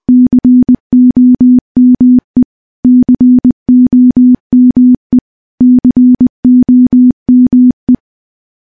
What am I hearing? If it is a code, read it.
Morse code, heard as a keyed tone, about 20 wpm, 250 Hz. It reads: COME COME COME
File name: morse / morse-training-20WPM